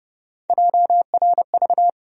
Morse code, heard as a keyed tone, about 29 wpm, 700 Hz. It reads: JRV
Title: USA Call Sign Postfixes - 3 Letters 30wpm